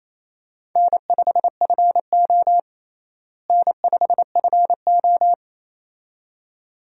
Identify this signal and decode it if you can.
Morse code, heard as a keyed tone, about 28 wpm, 700 Hz. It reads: N5FO N5FO
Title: USA Call Signs - 4 Characters - Repeated 2x - 28wpm